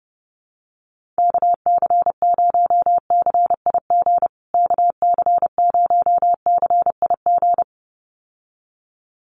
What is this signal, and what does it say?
Morse code, heard as a keyed tone, about 30 wpm, 700 Hz. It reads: KC0CIG KC0CIG
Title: USA Call Signs Repeated 2x 30wpm